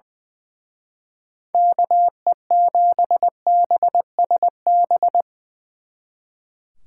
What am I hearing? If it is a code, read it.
Morse code, heard as a keyed tone, about 22 wpm, 700 Hz. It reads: KE7BSB
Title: Speed Racing - USA 2x3 Call Signs 20wpm